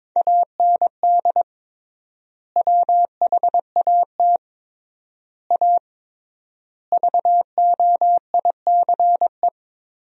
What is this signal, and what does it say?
Morse code, heard as a keyed tone, about 22 wpm, 700 Hz. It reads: AND WHAT A VOICE
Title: 3x Word-Spacing - Sentences from Top 500 Words - 22wpm